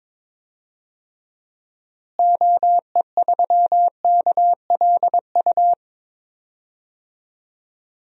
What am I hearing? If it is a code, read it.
Morse code, heard as a keyed tone, about 22 wpm, 700 Hz. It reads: OE3KLU
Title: DXCC-100 Call Signs - 6 Characters - Repeated 3x - 22wpm